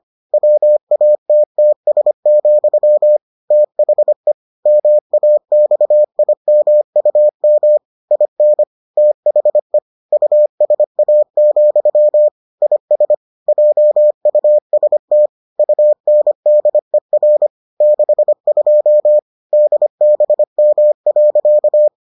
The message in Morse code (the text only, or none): WATTS, THE MAXIMUM IN THE USA, IS JUST UNDER 62 DBM.